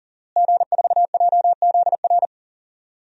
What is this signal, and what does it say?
Morse code, heard as a keyed tone, about 40 wpm, 700 Hz. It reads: G4JZR
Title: DXCC-100 Call Signs - 5 Characters - Repeated 3x - 40wpm